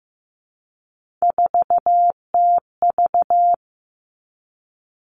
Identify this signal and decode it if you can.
Morse code, heard as a keyed tone, about 15 wpm, 700 Hz. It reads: 4TV
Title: Three Challenging Letter-Number Combinations 15wpm